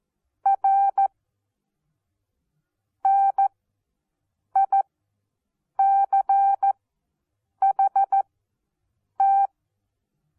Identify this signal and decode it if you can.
Morse code, heard as a keyed tone, about 14 wpm, spaced as for 7 wpm, 790 Hz. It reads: R NICHT
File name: Onlinelehrgang / morsen / DJ4UF Morsen14